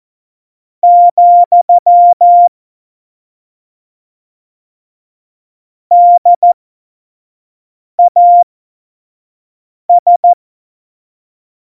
Morse code, heard as a keyed tone, about 14 wpm, spaced as for 5 wpm, 700 Hz. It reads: , DAS